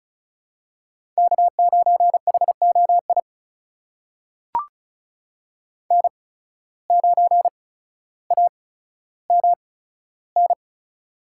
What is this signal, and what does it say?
Morse code, heard as a keyed tone, about 36 wpm, spaced as for 15 wpm, 700 Hz. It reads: K9HOI N9AMN
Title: USA Call Signs - 8x Character Spacing - 35wpm